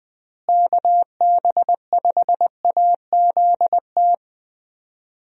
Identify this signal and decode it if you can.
Morse code, heard as a keyed tone, about 20 wpm, 700 Hz. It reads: KB5AZT
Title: USA Call Signs - 6 Characters - Repeated 3x - 20wpm